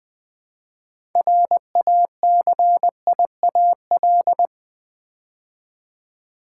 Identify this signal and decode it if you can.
Morse code, heard as a keyed tone, about 20 wpm, 700 Hz. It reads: RACIAL
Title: Build A Word - 2 Letters At A Time - 6-Letter Words - 20wpm